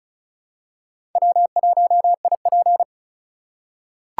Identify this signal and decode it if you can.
Morse code, heard as a keyed tone, about 36 wpm, 700 Hz. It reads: W1IP
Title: USA Call Signs - 4 Characters - 3x Character Spacing - 35wpm